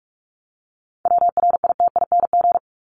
Morse code, heard as a keyed tone, about 44 wpm, 700 Hz. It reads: WRITING
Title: Top 751-1000 Words 45wpm